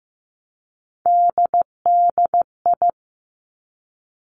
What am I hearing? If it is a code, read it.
Morse code, heard as a keyed tone, about 15 wpm, 700 Hz. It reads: DDI
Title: Three Challenging Letter-Number Combinations 15wpm